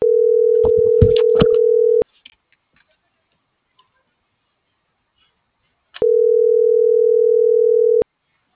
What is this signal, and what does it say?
Ringback tone (North America), 440 + 480 Hz.